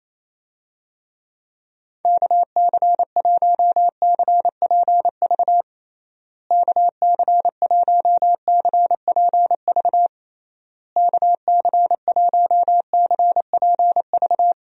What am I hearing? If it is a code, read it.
Morse code, heard as a keyed tone, about 28 wpm, 700 Hz. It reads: KC1CPV KC1CPV KC1CPV
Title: USA Call Signs - 6 Characters - Repeated 3x - 28wpm